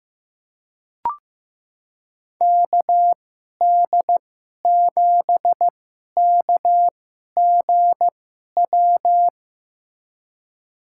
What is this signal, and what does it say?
Morse code, heard as a keyed tone, about 15 wpm, spaced as for 11 wpm, 700 Hz. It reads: KD7KGW
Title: USA Call Signs - 6 Characters - 2x Character Spacing - 15wpm